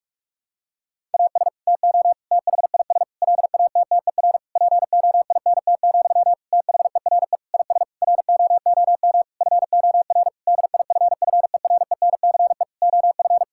Morse code, heard as a keyed tone, about 46 wpm, 700 Hz. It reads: AS TO THIS LATTER POINT, THERE IS ROOM FOR DIFFERENCE OF
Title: Sentences from Top 1600 Words 45wpm